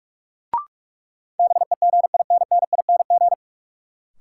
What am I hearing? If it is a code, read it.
Morse code, heard as a keyed tone, about 42 wpm, 700 Hz. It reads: BEGINNING
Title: Sentences from Top 800 Words - New Word Drill - 45wpm